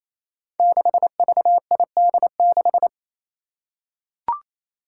Morse code, heard as a keyed tone, about 28 wpm, 700 Hz. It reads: 6VID6
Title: Five Challenging Letter-Number Combinations 28wpm